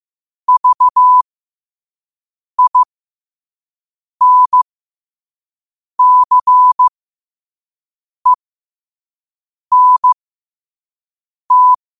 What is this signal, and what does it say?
Morse code, heard as a keyed tone, about 15 wpm, spaced as for 5 wpm, 1000 Hz. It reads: VINCENT
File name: qotdfiles / 5wpm